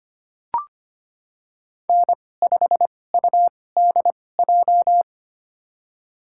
Morse code, heard as a keyed tone, about 25 wpm, spaced as for 18 wpm, 700 Hz. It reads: N5UDJ
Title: USA Call Signs - 5 Characters - 2x Character Spacing - 25wpm